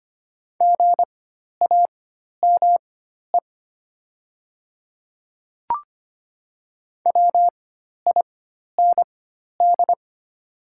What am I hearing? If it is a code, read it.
Morse code, heard as a keyed tone, about 25 wpm, spaced as for 12 wpm, 690 Hz. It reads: GAME WIND
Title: 4x Character Spacing - Top 500 Words - 25wpm